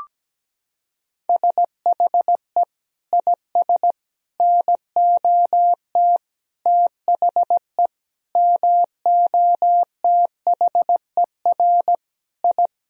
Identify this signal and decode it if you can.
Morse code, heard as a keyed tone, about 17 wpm, 700 Hz. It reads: SHE IS NOT THE MOTHER I
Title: Sentences from Top 1000 Words - Review Part 2 - 17wpm